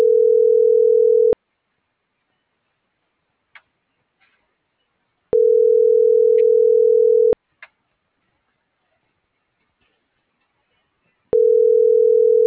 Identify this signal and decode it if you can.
Ringback tone (North America), 440 + 480 Hz.